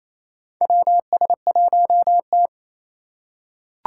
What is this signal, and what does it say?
Morse code, heard as a keyed tone, about 27 wpm, 700 Hz. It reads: WS1T
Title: USA Call Signs - 4 Characters - 3x Character Spacing - 28wpm